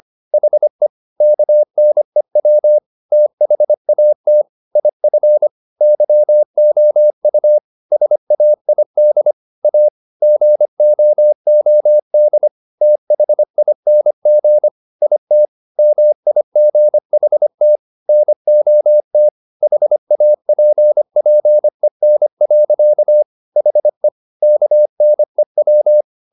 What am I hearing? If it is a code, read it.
Morse code, heard as a keyed tone, about 25 wpm, 600 Hz. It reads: HE KNEW THAT IF YOU SAID A GOOD THING IT MIGHT NOT HAPPEN. HE KNEW
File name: the old man and the sea part 1 25wpm